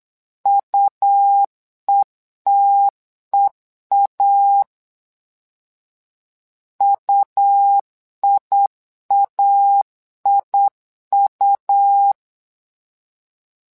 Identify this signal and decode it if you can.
Morse code, heard as a keyed tone, about 8 wpm, 800 Hz. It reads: UETEA UIAIU
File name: LEZIONI / 2 UHDN / 2 U 8wpm 2028